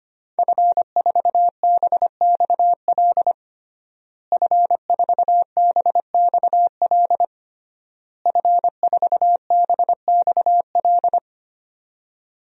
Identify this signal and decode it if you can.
Morse code, heard as a keyed tone, about 25 wpm, 700 Hz. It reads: F4BXL F4BXL F4BXL
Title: DXCC-100 Call Signs - 5 Characters - Repeated 3x - 25wpm